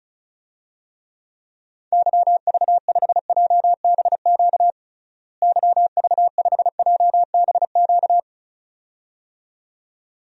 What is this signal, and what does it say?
Morse code, heard as a keyed tone, about 36 wpm, 700 Hz. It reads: YV5JBQ YV5JBQ
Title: DXCC-100 Call Signs - 6 Characters - Repeated 2x - 35wpm